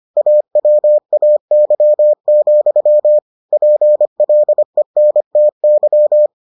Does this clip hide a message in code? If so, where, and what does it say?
Yes: Morse code from 0.2 to 6.3 s, heard as a keyed tone, about 25 wpm, 600 Hz. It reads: AWAY, PLENTY